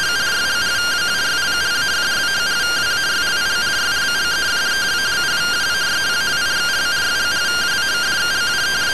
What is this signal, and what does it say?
Two-tone teleprinter signal, about 170 Hz shift.